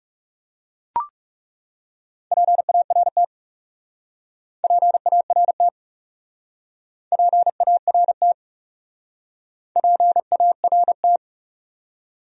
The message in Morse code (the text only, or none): PART PART PART PART